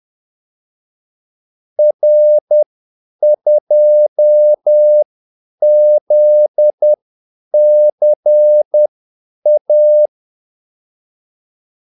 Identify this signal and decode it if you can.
Morse code, heard as a keyed tone, about 10 wpm, spaced as for 8 wpm, 600 Hz. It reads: R2ZCA
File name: Callsigns-10-8wpm